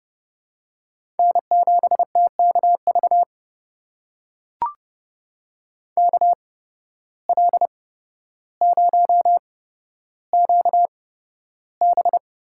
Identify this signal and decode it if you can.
Morse code, heard as a keyed tone, about 31 wpm, spaced as for 13 wpm, 700 Hz. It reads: N7TKV KL0QB
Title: USA Call Signs - 5 Characters - 8x Character Spacing - 30wpm